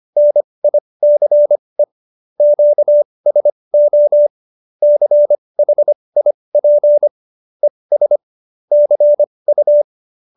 Morse code, heard as a keyed tone, about 25 wpm, spaced as for 20 wpm, 590 Hz. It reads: NICE QSO CHIP ES CU
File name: QSO316 20